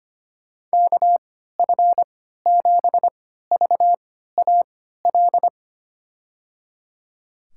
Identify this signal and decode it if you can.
Morse code, heard as a keyed tone, about 25 wpm, spaced as for 14 wpm, 700 Hz. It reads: KF7VAL
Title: USA Call Signs - 6 Characters - 3x Character Spacing - 25wpm